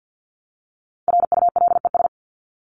Morse code, heard as a keyed tone, about 50 wpm, 700 Hz. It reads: RULES
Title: ES:S Suffix - Top 100 Words with Base Words 50wpm